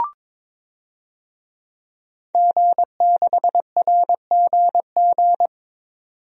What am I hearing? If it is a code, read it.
Morse code, heard as a keyed tone, about 22 wpm, 700 Hz. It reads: G6RGG
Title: DXCC-100 Call Signs - 5 Characters - 22wpm